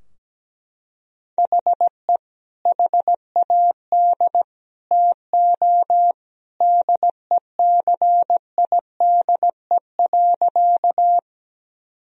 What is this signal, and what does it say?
Morse code, heard as a keyed tone, about 17 wpm, 700 Hz. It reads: HE HAD TO DECIDE.
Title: Sentences from Top 1000 Words - Review Part 3 - 17wpm